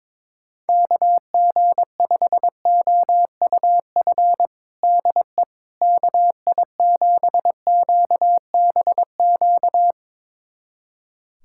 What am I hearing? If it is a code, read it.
Morse code, heard as a keyed tone, about 21 wpm, 700 Hz. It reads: KG5OUF DE KI7QBQ
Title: USA Callsign DE Callsign 22wpm